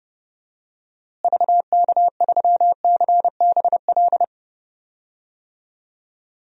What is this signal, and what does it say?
Morse code, heard as a keyed tone, about 30 wpm, 700 Hz. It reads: VK3CBL
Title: DXCC-100 Call Signs - 6 Characters - Repeated 2x - 30wpm